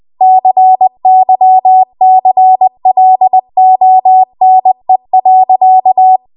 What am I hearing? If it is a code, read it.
Morse code, heard as a keyed tone, about 20 wpm, 750 Hz. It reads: CYCLONE.
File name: Archive / 20 WPM / 20WPM